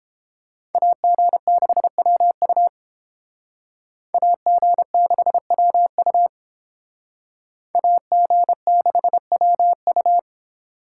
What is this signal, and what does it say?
Morse code, heard as a keyed tone, about 30 wpm, 700 Hz. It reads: AG6WU AG6WU AG6WU